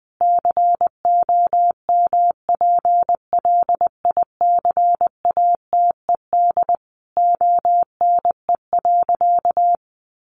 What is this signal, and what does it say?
Morse code, heard as a keyed tone, about 20 wpm, 700 Hz. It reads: COMPLICATED ONE.
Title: Sentences from Top 1200 Words 20wpm